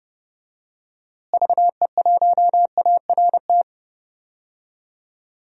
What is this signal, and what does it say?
Morse code, heard as a keyed tone, about 30 wpm, 700 Hz. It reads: VE1ART
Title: DXCC-100 Call Signs - 6 Characters - Repeated 3x - 30wpm